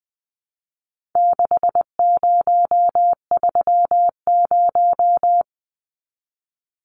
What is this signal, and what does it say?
Morse code, heard as a keyed tone, about 20 wpm, 700 Hz. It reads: 6030
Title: Numbers - 4-Digit 20wpm